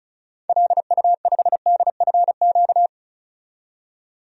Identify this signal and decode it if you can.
Morse code, heard as a keyed tone, about 36 wpm, 700 Hz. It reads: LU5DFQ
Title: DXCC-100 Call Signs - 6 Characters - 35wpm